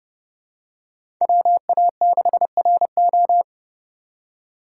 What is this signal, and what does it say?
Morse code, heard as a keyed tone, about 30 wpm, 700 Hz. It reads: WA6RO
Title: USA Call Signs - 8x Character Spacing - 30wpm